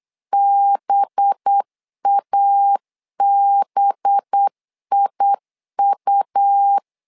Morse code, heard as a keyed tone, about 8 wpm, 800 Hz. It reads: BABIU
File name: LEZIONI / 5 LBVC / 5 LB 8wpm 2816